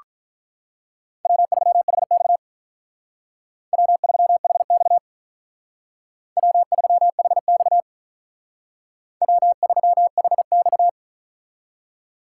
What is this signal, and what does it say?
Morse code, heard as a keyed tone, about 45 wpm, 700 Hz. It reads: W3HX W3HX W3HX W3HX